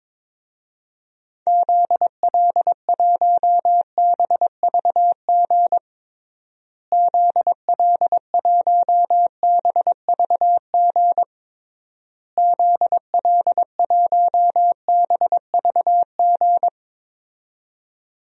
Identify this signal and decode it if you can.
Morse code, heard as a keyed tone, about 22 wpm, 700 Hz. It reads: ZL1BVG ZL1BVG ZL1BVG
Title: DXCC-100 Call Signs - 6 Characters - Repeated 3x - 22wpm